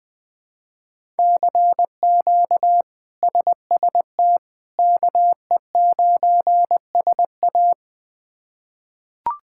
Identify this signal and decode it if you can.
Morse code, heard as a keyed tone, about 20 wpm, 700 Hz. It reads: CQ SST KE9SA
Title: K1USN SST QSO Practice - 20wpm